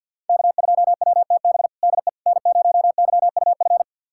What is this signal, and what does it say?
Morse code, heard as a keyed tone, about 49 wpm, 700 Hz. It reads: K2WTB DE N0YUF
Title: USA Callsign DE Callsign - 5 Characters - 50wpm